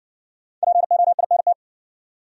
Morse code, heard as a keyed tone, about 58 wpm, 700 Hz. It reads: POINT